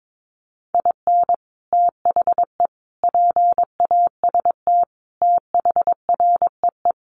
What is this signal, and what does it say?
Morse code, heard as a keyed tone, about 22 wpm, 700 Hz. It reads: IN THE PAST THREE
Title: Sets of 5 Words Encore2 22wpm